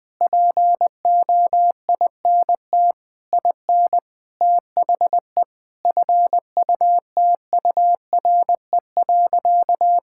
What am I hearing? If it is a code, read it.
Morse code, heard as a keyed tone, about 20 wpm, 700 Hz. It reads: POINT IN THE FUTURE.
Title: Sentences with 6-Word Phrases - 20wpm